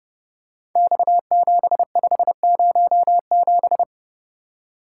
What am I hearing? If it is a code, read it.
Morse code, heard as a keyed tone, about 30 wpm, 700 Hz. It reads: X7507
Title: Code Groups - 5 Letter:Numbers - 30wpm